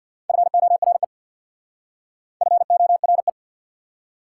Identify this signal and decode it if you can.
Morse code, heard as a keyed tone, about 61 wpm, 700 Hz. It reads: FORE FORE
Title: Speed-Racing Prefixes and Suffixes 50wpm